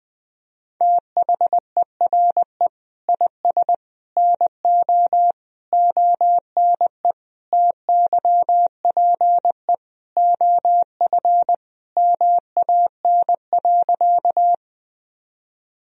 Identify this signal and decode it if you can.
Morse code, heard as a keyed tone, about 21 wpm, 700 Hz. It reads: THERE IS NO ONE TYPE OF MAN.
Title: Sentences from Top 1000 Words - Review Part 3 - 20wpm